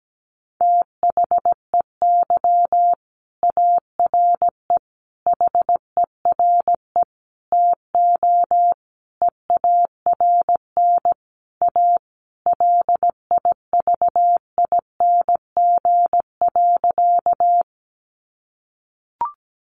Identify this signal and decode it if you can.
Morse code, heard as a keyed tone, about 17 wpm, 700 Hz. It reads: THEY ARE HERE TO EARN A LIVING.
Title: Sentences from Top 1500 Words 17wpm